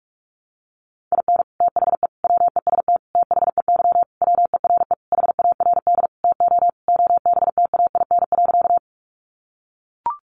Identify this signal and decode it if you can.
Morse code, heard as a keyed tone, about 44 wpm, 700 Hz. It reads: IN THE WEST THEY WERE HARD TO OBTAIN.
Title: Sentences from Top 1300 Words 45wpm